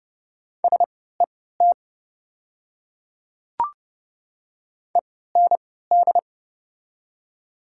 Morse code, heard as a keyed tone, about 30 wpm, spaced as for 17 wpm, 700 Hz. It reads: SET END
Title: 3x Character Spacing - Top 300 Words - 30wpm